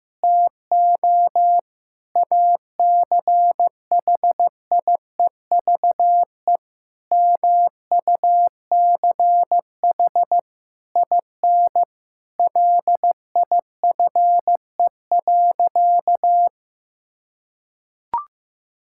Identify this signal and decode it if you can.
Morse code, heard as a keyed tone, about 15 wpm, 700 Hz. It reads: TO ACHIEVE MUCH IN LIFE.
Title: Sentences from Top 1000 Words - Review Part 6 - 15wpm